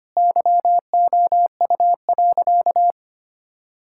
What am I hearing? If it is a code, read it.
Morse code, heard as a keyed tone, about 25 wpm, 700 Hz. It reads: YOU.
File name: practice / 25wpm / Sentences with 3-Word Phrases - 25wpm